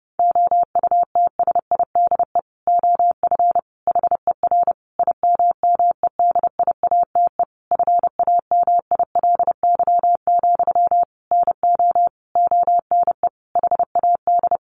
Morse code, heard as a keyed tone, about 30 wpm, 700 Hz. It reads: OUTSIDE OF HER IMMEDIATE FAMILY, NO ONE HAD
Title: Sentences from Top 1200 Words 30wpm